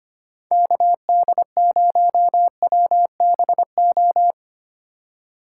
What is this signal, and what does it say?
Morse code, heard as a keyed tone, about 32 wpm, 700 Hz. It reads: KD0WBO